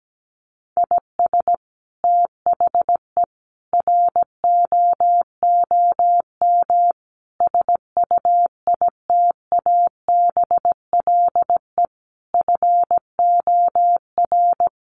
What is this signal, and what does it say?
Morse code, heard as a keyed tone, about 17 wpm, 700 Hz. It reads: IS THE ROOM SUITABLE FOR
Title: Sentences from Top 1400 Words 17wpm